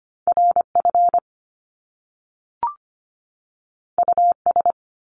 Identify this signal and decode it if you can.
Morse code, heard as a keyed tone, about 25 wpm, 700 Hz. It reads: RF US
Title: Top Two Letter Combinations in Words 25wpm